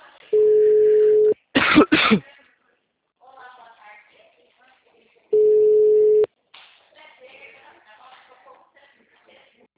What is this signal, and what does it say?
Ringback tone (Europe, 425 Hz), with speech.